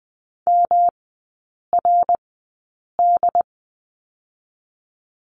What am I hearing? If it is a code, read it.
Morse code, heard as a keyed tone, about 20 wpm, 700 Hz. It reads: M R D
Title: 114 Lesson 19e - Letter w - Three Characters 20wpm